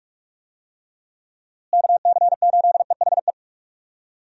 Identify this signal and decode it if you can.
Morse code, heard as a keyed tone, about 46 wpm, 700 Hz. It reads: KC8EHE
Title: USA Call Signs - 6 Characters - Repeated 3x - 45wpm